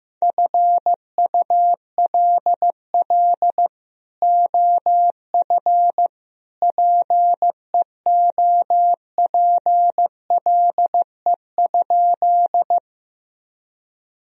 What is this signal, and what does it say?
Morse code, heard as a keyed tone, about 15 wpm, 700 Hz. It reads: FULL OF PEOPLE?
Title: Sentences from Top 1000 Words - Review Part 3 - 15wpm